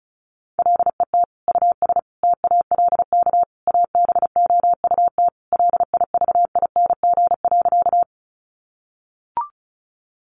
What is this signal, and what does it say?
Morse code, heard as a keyed tone, about 34 wpm, 700 Hz. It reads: LET US TALK ABOUT LIVING.
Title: Sentences from Top 1100 Words 35wpm